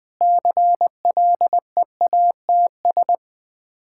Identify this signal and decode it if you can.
Morse code, heard as a keyed tone, about 20 wpm, 700 Hz. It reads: CLEATS
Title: Difficult Words with Hints - 20wpm